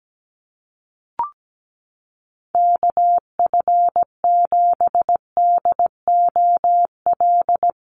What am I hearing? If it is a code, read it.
Morse code, heard as a keyed tone, about 17 wpm, 700 Hz. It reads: KF7DOL
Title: USA Call Signs Repeated 2x 17wpm